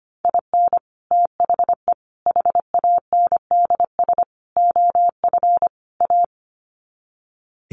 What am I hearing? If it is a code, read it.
Morse code, heard as a keyed tone, about 25 wpm, 700 Hz. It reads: IN THE HANDS OF A
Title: Sets of 5 Words Encore 25wpm